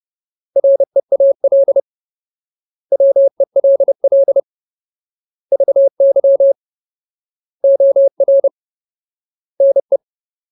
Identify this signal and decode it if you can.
Morse code, heard as a keyed tone, about 30 wpm, 550 Hz. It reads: REAL WELL VY OR NE
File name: Ultimate Word List / 30WPM Ultimate